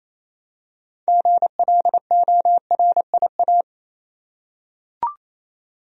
Morse code, heard as a keyed tone, about 28 wpm, 700 Hz. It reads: GLORIA
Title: USA Common Names - 28wpm